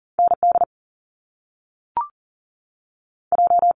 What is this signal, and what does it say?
Morse code, heard as a keyed tone, about 38 wpm, 700 Hz. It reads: ND J
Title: Top Two Letter Combinations in Words 40wpm